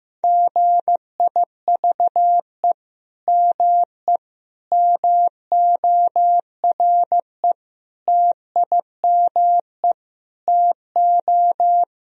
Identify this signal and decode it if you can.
Morse code, heard as a keyed tone, about 15 wpm, 700 Hz. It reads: GIVE ME MORE TIME TO
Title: Sentences from Top 1000 Words - Review Part 3 - 15wpm